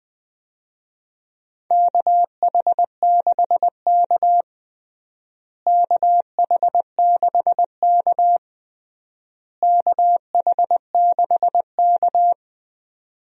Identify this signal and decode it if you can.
Morse code, heard as a keyed tone, about 21 wpm, 700 Hz. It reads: KH6K KH6K KH6K
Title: DXCC-100 Call Signs - 4 Characters - Repeated 3x - 20wpm